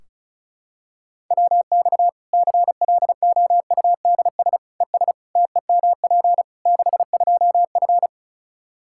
Morse code, heard as a keyed tone, about 36 wpm, 700 Hz. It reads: WX CLOUDS ES TEMP 62F
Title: K1USN SST QSO Practice 35wpm